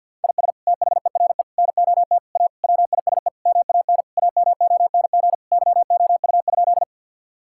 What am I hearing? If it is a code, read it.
Morse code, heard as a keyed tone, about 49 wpm, 700 Hz. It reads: IS THERE NOT A WISE MAN AMONG YOU?
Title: Sentences from Top 1600 Words 50wpm